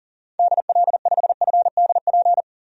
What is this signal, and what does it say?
Morse code, heard as a keyed tone, about 40 wpm, 700 Hz. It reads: DL5FDP
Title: DXCC-100 Call Signs - 6 Characters - 40wpm